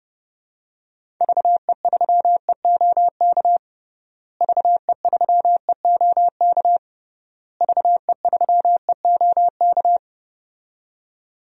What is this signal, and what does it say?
Morse code, heard as a keyed tone, about 29 wpm, 700 Hz. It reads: VE3EOK VE3EOK VE3EOK
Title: DXCC-100 Call Signs - 6 Characters - Repeated 3x - 30wpm